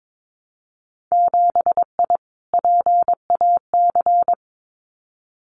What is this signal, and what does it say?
Morse code, heard as a keyed tone, about 22 wpm, 700 Hz. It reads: 7I PAC
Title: Winter Field Day Exchanges Encore 22wpm